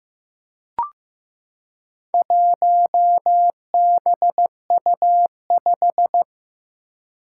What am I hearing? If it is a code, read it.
Morse code, heard as a keyed tone, about 15 wpm, 700 Hz. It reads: 1BU5
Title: Four Challenging Letter-Number Combinations 15wpm